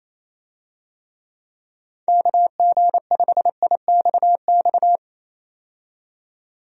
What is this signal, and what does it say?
Morse code, heard as a keyed tone, about 27 wpm, 700 Hz. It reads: KG5IXX
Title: USA Call Signs - 6 Characters - 28wpm